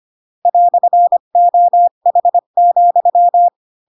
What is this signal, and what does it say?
Morse code, heard as a keyed tone, about 25 wpm, 700 Hz. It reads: "OH,
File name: Pinocchio / Pinocchio ch7 20 25 ik1hge